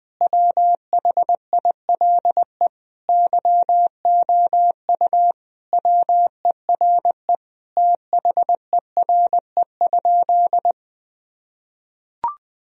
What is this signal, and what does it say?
Morse code, heard as a keyed tone, about 20 wpm, 700 Hz. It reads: WHILE YOU WERE THERE?
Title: Sentences from Top 1000 Words - Review Part 1 - 20wpm